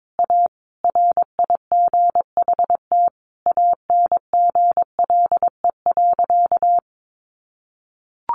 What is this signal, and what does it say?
Morse code, heard as a keyed tone, about 22 wpm, 700 Hz. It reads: A RIGHT ANGLE.
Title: Sentences from Top 1500 Words 22wpm